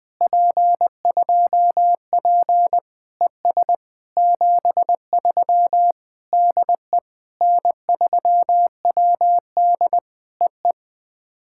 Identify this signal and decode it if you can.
Morse code, heard as a keyed tone, about 21 wpm, 700 Hz. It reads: P2P ES 73 DE N3WD EE
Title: POTA QSO Practice - 20wpm